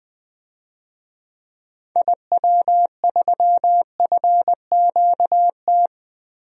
Morse code, heard as a keyed tone, about 21 wpm, 700 Hz. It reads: IW3FQT
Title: DXCC-100 Call Signs - 6 Characters - Repeated 3x - 20wpm